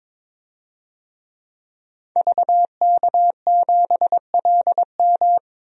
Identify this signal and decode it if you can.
Morse code, heard as a keyed tone, about 22 wpm, 700 Hz. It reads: VK7LM
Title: DXCC-100 Call Signs - 5 Characters - 22wpm